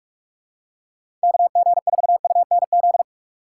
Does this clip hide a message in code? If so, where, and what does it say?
Yes: Morse code from 1.2 to 3.0 s, heard as a keyed tone, about 49 wpm, 700 Hz. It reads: KG4UNZ